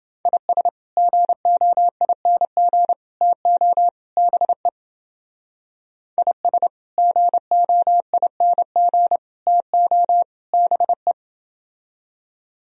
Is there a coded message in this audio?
Yes: Morse code from 0.2 to 11.1 s, heard as a keyed tone, about 28 wpm, 700 Hz. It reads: IS GOING TO BE IS GOING TO BE